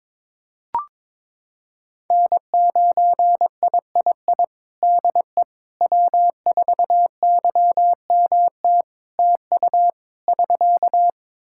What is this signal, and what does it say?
Morse code, heard as a keyed tone, about 22 wpm, 700 Hz. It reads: N9III DE W4YMT TU <SK>
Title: Practice QSO - Exchange 4 - Ending - 22wpm